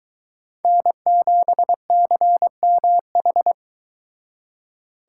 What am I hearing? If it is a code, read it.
Morse code, heard as a keyed tone, about 23 wpm, 700 Hz. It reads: N7CMH